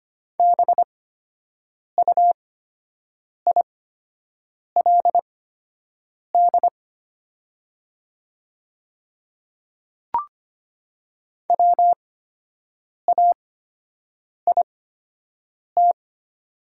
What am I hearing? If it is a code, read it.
Morse code, heard as a keyed tone, about 25 wpm, spaced as for 7 wpm, 690 Hz. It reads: BUILD WAIT
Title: 8x Character Spacing - Top 500 Words - 25wpm